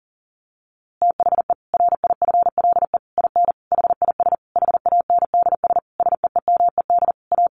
Morse code, heard as a keyed tone, about 37 wpm, 700 Hz. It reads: THE RIFLE IN HIS HANDS SEEMED A
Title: A Sound Of Thunder 40wpm